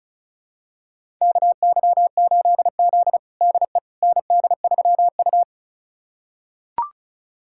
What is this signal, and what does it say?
Morse code, heard as a keyed tone, about 34 wpm, 700 Hz. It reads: KY8Z DE ND3U
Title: USA Callsign DE Callsign - 4 Characters - 35wpm